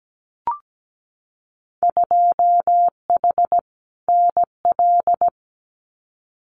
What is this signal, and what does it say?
Morse code, heard as a keyed tone, about 17 wpm, 700 Hz. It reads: 2H NL
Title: Winter Field Day Exchanges Encore 17wpm